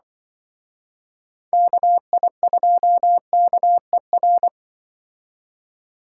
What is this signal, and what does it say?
Morse code, heard as a keyed tone, about 24 wpm, 700 Hz. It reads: KI2KER
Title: Speed Racing - USA Call Signs 20wpm